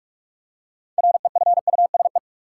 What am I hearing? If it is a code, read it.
Morse code, heard as a keyed tone, about 46 wpm, 700 Hz. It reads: REFUSE
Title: Six Letter Words - 45wpm